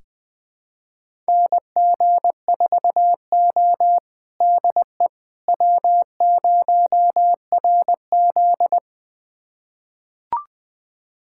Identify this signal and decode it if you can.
Morse code, heard as a keyed tone, about 20 wpm, 700 Hz. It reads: NG4O DE W0RZ
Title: USA Callsign DE Callsign - 4 Characters - 20wpm